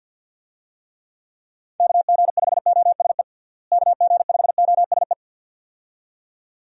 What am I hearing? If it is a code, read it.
Morse code, heard as a keyed tone, about 47 wpm, 700 Hz. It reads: KG5OSE KG5OSE
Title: USA Call Signs - 6 Characters - Repeated 2x - 50wpm